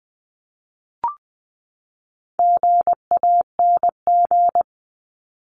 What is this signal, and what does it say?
Morse code, heard as a keyed tone, about 20 wpm, 700 Hz. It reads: GANG
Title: 260 Lesson 42e - Words 20-20wpm